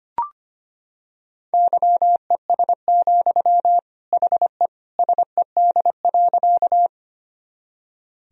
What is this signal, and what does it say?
Morse code, heard as a keyed tone, about 25 wpm, 700 Hz. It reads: YES, HE SED.
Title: Sentences from Top 500 Words and Abbreviations 25wpm